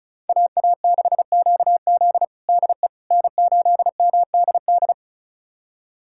Morse code, heard as a keyed tone, about 34 wpm, 700 Hz. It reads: AA6QZ DE N8MDD
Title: USA Callsign DE Callsign - 5 Characters - 35wpm